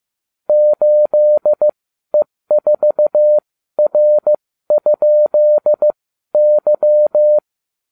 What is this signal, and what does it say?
Morse code, heard as a keyed tone, about 15 wpm, spaced as for 12 wpm, 600 Hz. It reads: 8E4R?Y